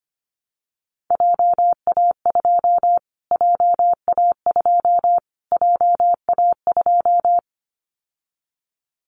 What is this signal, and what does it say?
Morse code, heard as a keyed tone, about 25 wpm, 700 Hz. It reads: JA2 JA2 JA2
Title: Callsign Prefixes for Top 25 DXCC - Repeated 3x - 25wpm